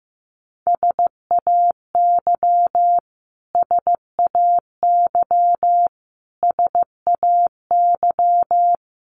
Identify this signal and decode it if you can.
Morse code, heard as a keyed tone, about 15 wpm, 700 Hz. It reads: SAY SAY SAY
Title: Top 500 Words Repeated 15wpm